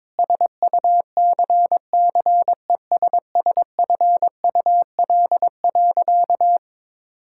Morse code, heard as a keyed tone, about 22 wpm, 700 Hz. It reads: SUCCESSFUL.
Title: Sentences from Top 1000 Words - Review Part 6 - 22wpm